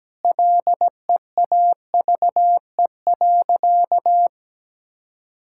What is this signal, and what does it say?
Morse code, heard as a keyed tone, about 17 wpm, 700 Hz. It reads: LEAVE.
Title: Sentences from Top 1000 Words - Review Part 6 - 17wpm